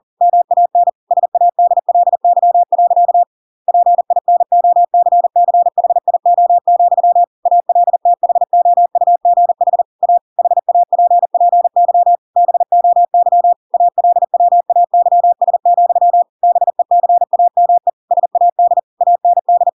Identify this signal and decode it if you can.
Morse code, heard as a keyed tone, about 40 wpm, 700 Hz. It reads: MAN SADLY. PINOCCHIO, ALTHOUGH A HAPPY BOY ALWAYS, BECAME SAD AND
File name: Pinocchio / Pinocchio ch8 40 40 ik1hge